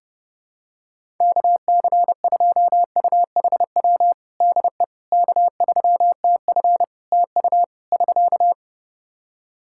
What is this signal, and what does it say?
Morse code, heard as a keyed tone, about 30 wpm, 700 Hz. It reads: KC2UHW DE K3TF TU <SK>
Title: Practice QSO - Exchange 4 - Ending - 30wpm